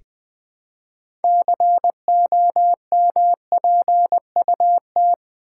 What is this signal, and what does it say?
Morse code, heard as a keyed tone, about 20 wpm, 700 Hz. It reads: COMPUT
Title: Build A Word - 2 Letters At A Time - 8-Letter Words - 20wpm